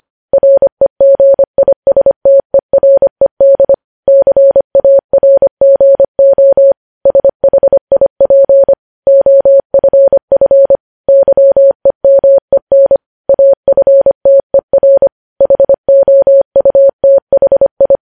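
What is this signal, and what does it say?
Morse code, heard as a keyed tone, about 25 wpm, 550 Hz. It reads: REGISTERED CARGO SHIP OFF YEMEN AFTER HOUTHI
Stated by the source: Morse news headlines at 25 WPM.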